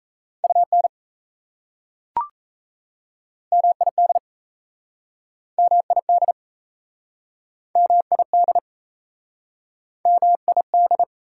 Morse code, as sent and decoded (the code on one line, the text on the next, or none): ..- -. / -- .. -.. / -- .. -.. / -- .. -.. / -- .. -..
UN MID MID MID MID